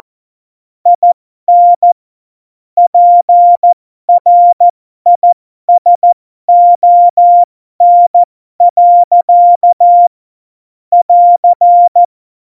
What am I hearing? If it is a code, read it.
Morse code, heard as a keyed tone, about 14 wpm, spaced as for 12 wpm, 700 Hz. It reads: IN PRISON. <AR>
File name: Morsekurs / Witze